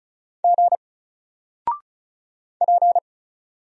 Morse code, heard as a keyed tone, about 34 wpm, 700 Hz. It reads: G P
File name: Single-Letter-Number-35wpm